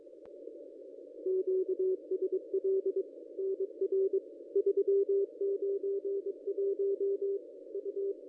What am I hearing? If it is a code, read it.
Morse code, heard as a keyed tone, about 24 wpm, 410 Hz. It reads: QSL NR 391 U
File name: contests / ss-cw-k5tr-op-n5rz -2200